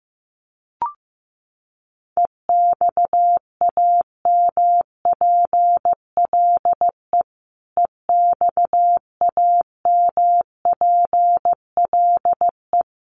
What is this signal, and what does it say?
Morse code, heard as a keyed tone, about 15 wpm, 700 Hz. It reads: EXAMPLE EXAMPLE
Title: Top 500 Words Repeated 15wpm